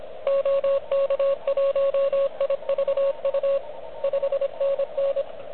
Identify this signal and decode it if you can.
Morse code, heard as a keyed tone, about 26 wpm, 560 Hz. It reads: OK1IVU 5NN
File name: OK QRP 2022 / Zvuk / REC34